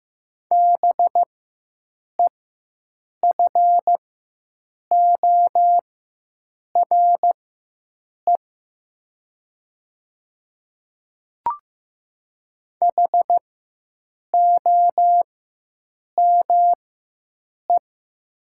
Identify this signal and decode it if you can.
Morse code, heard as a keyed tone, about 15 wpm, spaced as for 7 wpm, 700 Hz. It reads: BEFORE HOME
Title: 4x Character Spacing - Top 200 Words - 15wpm